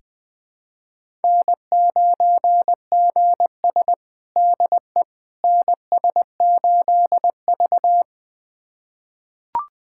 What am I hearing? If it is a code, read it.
Morse code, heard as a keyed tone, about 21 wpm, 700 Hz. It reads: N9GS DE NS8V
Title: USA Callsign DE Callsign - 4 Characters - 20wpm